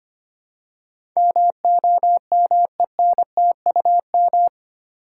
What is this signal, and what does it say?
Morse code, heard as a keyed tone, about 25 wpm, 700 Hz. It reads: MOMENTUM
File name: Eight Letter Words - 25wpm